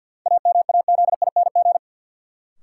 Morse code, heard as a keyed tone, about 53 wpm, 700 Hz. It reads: AMAZING
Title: Sentences from Top 1500 Words - New Word Drill - 50wpm